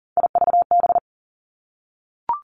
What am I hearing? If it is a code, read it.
Morse code, heard as a keyed tone, about 41 wpm, 700 Hz. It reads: IVB
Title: Three Challenging Letter-Number Combinations 40wpm